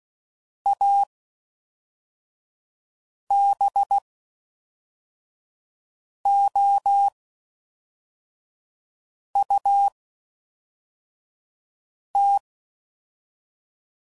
Morse code, heard as a keyed tone, about 16 wpm, spaced as for 4 wpm, 800 Hz. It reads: ABOUT